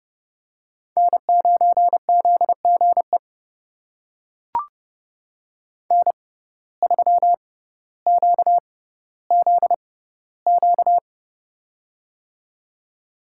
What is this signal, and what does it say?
Morse code, heard as a keyed tone, about 30 wpm, spaced as for 15 wpm, 700 Hz. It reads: N9ZGE N3QZQ
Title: USA Call Signs - 5 Characters - 6x Character Spacing - 30wpm